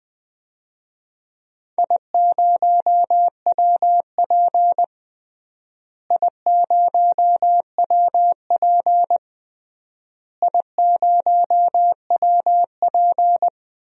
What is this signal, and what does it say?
Morse code, heard as a keyed tone, about 21 wpm, 700 Hz. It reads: I0WP I0WP I0WP
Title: DXCC-100 Call Signs - 4 Characters - Repeated 3x - 20wpm